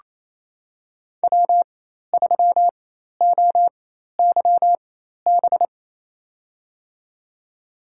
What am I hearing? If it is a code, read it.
Morse code, heard as a keyed tone, about 28 wpm, spaced as for 13 wpm, 700 Hz. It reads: W3OYB
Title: USA Call Signs - 5 Characters - 4x Character Spacing - 28wpm